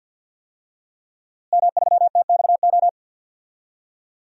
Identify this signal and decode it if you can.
Morse code, heard as a keyed tone, about 50 wpm, 700 Hz. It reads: M3TXO
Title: DXCC-100 Call Signs - 5 Characters - Repeated 2x - 50wpm